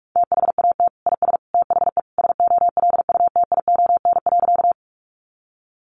Morse code, heard as a keyed tone, about 46 wpm, 700 Hz. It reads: THAT IS THE SOLUTION.
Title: Sentences from Top 1300 Words 45wpm